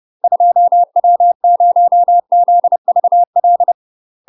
Morse code, heard as a keyed tone, about 30 wpm, 700 Hz. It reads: 2W0ZVL
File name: UK MASTER QSOs TRAINING / 30WPM / UK MASTER QSOs 0044